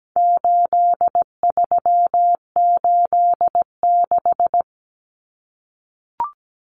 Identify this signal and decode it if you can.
Morse code, heard as a keyed tone, about 17 wpm, 700 Hz. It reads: 8386
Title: Numbers - 4-Digit 17wpm